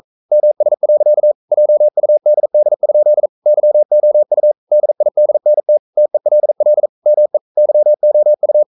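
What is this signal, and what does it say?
Morse code, heard as a keyed tone, about 42 wpm, 600 Hz. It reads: MS. JUDD? YOU DIDNT TELL ME YOU
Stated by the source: Morse code sending of an English novel at 42 wpm.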